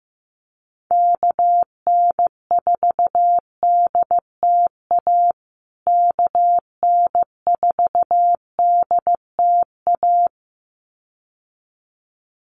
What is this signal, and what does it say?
Morse code, heard as a keyed tone, about 15 wpm, 700 Hz. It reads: KN4DTA KN4DTA
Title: USA Call Signs Repeated 2x 15wpm